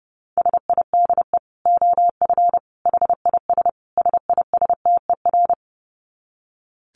Morse code, heard as a keyed tone, about 30 wpm, 700 Hz. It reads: SIDE OF HIS SISTER
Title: Dit Challenge 30wpm